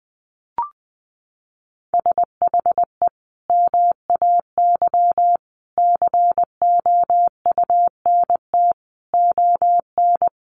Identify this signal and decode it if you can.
Morse code, heard as a keyed tone, about 20 wpm, 700 Hz. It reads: SHE MAY COUNT ON
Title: Sentences from Top 1500 Words 20wpm